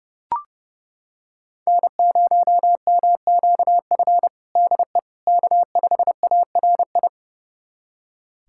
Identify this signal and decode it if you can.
Morse code, heard as a keyed tone, about 32 wpm, 700 Hz. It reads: N0MQF DE K5ARI
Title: USA Callsign DE Callsign 30wpm